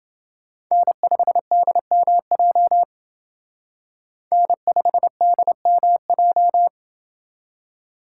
Morse code, heard as a keyed tone, about 28 wpm, 700 Hz. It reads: N5DMJ N5DMJ